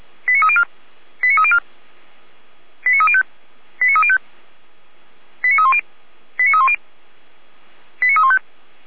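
Five-tone selective call (ZVEI), heard as quick runs of five tones, tones of about 70 ms. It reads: 89304 89304 89305 89305 89310 89310 89315
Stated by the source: Five-tone ZVEI selective call 89303 received by SDR.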